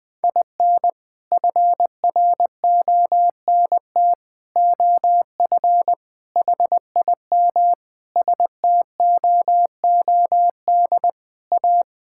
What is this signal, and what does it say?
Morse code, heard as a keyed tone, about 21 wpm, 690 Hz. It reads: IN FRONT OF HIM STOOD A
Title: Sentences with 4-Word Phrases - Spoken Beginning - 20wpm